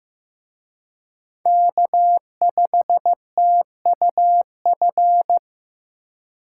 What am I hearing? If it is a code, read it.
Morse code, heard as a keyed tone, about 18 wpm, 700 Hz. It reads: K5TUF
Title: Speed Racing - USA 1x3 Call Signs 15wpm